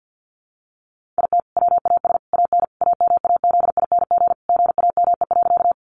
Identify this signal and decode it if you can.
Morse code, heard as a keyed tone, about 50 wpm, 700 Hz. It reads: IT WAS AN AMAZING GAME.
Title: Sentences from Top 1500 Words 50wpm